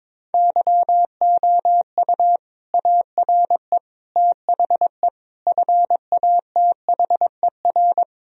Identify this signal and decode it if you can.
Morse code, heard as a keyed tone, about 22 wpm, 700 Hz. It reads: YOU ARE THE FATHER
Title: Sentences from Top 1000 Words - Review Part 2 - 22wpm